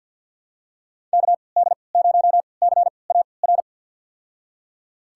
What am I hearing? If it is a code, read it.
Morse code, heard as a keyed tone, about 49 wpm, spaced as for 28 wpm, 700 Hz. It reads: KD0CAR
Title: USA Call Signs - 6 Characters - 3x Character Spacing - 50wpm